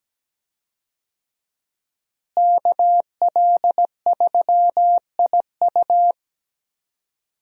Morse code, heard as a keyed tone, about 17 wpm, 700 Hz. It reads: KL3IU
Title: USA Call Signs - 5 Characters - Repeated 3x - 17wpm